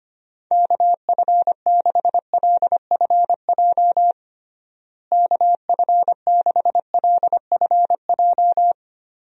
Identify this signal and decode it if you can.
Morse code, heard as a keyed tone, about 25 wpm, 700 Hz. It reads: KF6LFJ KF6LFJ
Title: USA Call Signs - 6 Characters - Repeated 2x - 25wpm